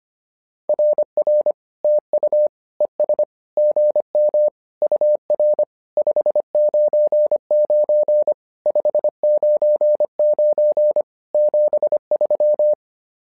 Morse code, heard as a keyed tone, about 25 wpm, 600 Hz. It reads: RR TU ES GM UR 599 599 73
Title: SOTA QSO Practice 25wpm